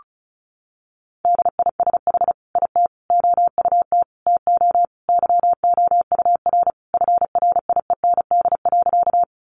Morse code, heard as a keyed tone, about 34 wpm, 700 Hz. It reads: DISH IT OUT TO YOUR FRIEND.
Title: Sentences from Top 1300 Words 35wpm